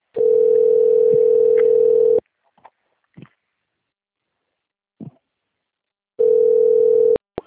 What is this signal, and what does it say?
Ringback tone (North America), 440 + 480 Hz.